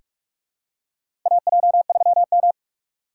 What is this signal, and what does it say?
Morse code, heard as a keyed tone, about 41 wpm, 700 Hz. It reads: AJ3M
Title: USA Call Signs - 6x Character Spacing - 45wpm